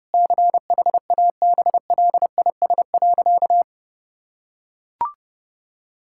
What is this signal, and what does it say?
Morse code, heard as a keyed tone, about 32 wpm, 700 Hz. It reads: CHABLIS.
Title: Pangram Challenge v2 - 30wpm